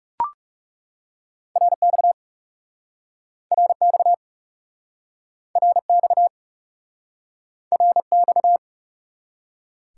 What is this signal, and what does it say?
Morse code, heard as a keyed tone, about 40 wpm, 700 Hz. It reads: RX RX RX RX